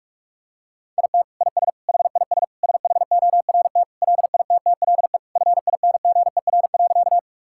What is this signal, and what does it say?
Morse code, heard as a keyed tone, about 46 wpm, 700 Hz. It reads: IT IS HIS SHORT LITTLE FINGER.
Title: Sentences from Top 1600 Words 45wpm